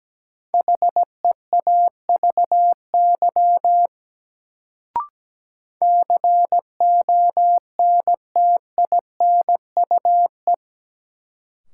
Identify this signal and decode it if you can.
Morse code, heard as a keyed tone, about 17 wpm, 700 Hz. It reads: HEAVY CONTINUE
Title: Sentences from Top 1000 Words - Review Part 6 - New Word Drill - 17wpm